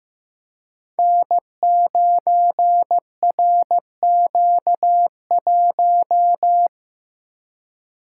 Morse code, heard as a keyed tone, about 15 wpm, 700 Hz. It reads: N9RQ1
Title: Code Groups - 5 Letter:Numbers - 15wpm